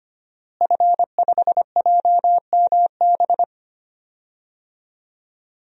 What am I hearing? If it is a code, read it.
Morse code, heard as a keyed tone, about 25 wpm, 700 Hz. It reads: F5JMB
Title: DXCC-100 Call Signs - 5 Characters - Repeated 2x - 25wpm